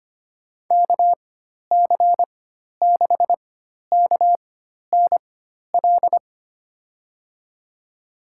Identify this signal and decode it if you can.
Morse code, heard as a keyed tone, about 25 wpm, spaced as for 12 wpm, 700 Hz. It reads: KC6KNL
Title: USA Call Signs - 6 Characters - 4x Character Spacing - 25wpm